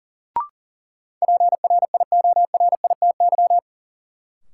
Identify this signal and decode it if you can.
Morse code, heard as a keyed tone, about 40 wpm, 700 Hz. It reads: PRIORITY
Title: Sentences from Top 1600 Words - New Word Drill - 40wpm